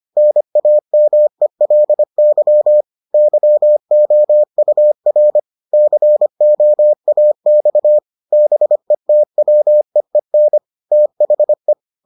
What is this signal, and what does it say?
Morse code, heard as a keyed tone, about 25 wpm, 600 Hz. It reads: NAMELY YOUR COAX BETWEEN THE